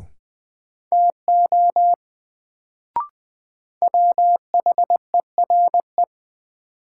Morse code, heard as a keyed tone, about 20 wpm, 700 Hz. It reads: TO WHERE
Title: Sets of 2 Words - New Word Drill - 20wpm